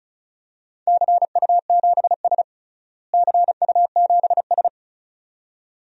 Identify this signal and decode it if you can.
Morse code, heard as a keyed tone, about 36 wpm, 690 Hz. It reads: CU7S CU7S
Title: DXCC-100 Call Signs - 4 Characters - Repeated 2x - 35wpm